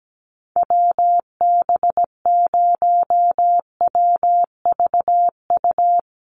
Morse code, heard as a keyed tone, about 17 wpm, 700 Hz. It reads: WB0WVU
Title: USA Call Signs Repeated 2x 17wpm